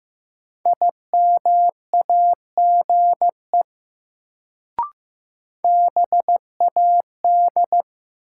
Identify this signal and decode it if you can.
Morse code, heard as a keyed tone, about 15 wpm, 700 Hz. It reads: IMAGE BAD
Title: Sentences from Top 700 Words - New Word Drill - 15wpm